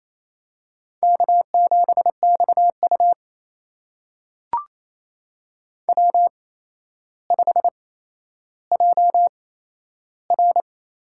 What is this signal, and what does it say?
Morse code, heard as a keyed tone, about 28 wpm, spaced as for 12 wpm, 700 Hz. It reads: K7XU W5JR
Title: USA Call Signs - 4 Characters - 8x Character Spacing - 28wpm